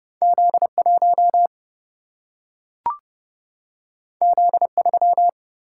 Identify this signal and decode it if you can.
Morse code, heard as a keyed tone, about 30 wpm, 700 Hz. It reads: Z1 Z3
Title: Single-Letter Single-Number 30wpm